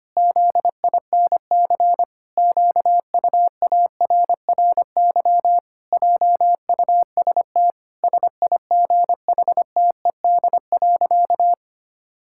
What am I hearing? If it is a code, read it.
Morse code, heard as a keyed tone, about 25 wpm, 700 Hz. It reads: ZINC QUARRY JUST SIGHTED.
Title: Pangram Challenge v2 - 25wpm